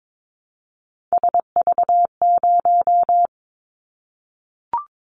Morse code, heard as a keyed tone, about 22 wpm, 700 Hz. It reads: SV0
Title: Callsign Prefixes for Top 50 DXCC - 22wpm